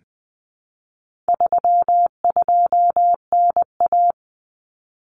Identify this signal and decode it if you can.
Morse code, heard as a keyed tone, about 20 wpm, 700 Hz. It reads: 32NA
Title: Common Cut Numbers - 4-Digit 20wpm